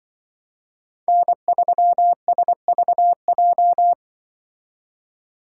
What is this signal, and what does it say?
Morse code, heard as a keyed tone, about 24 wpm, 700 Hz. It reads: N3SVJ
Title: Speed Racing - USA 1x3 Call Signs 20wpm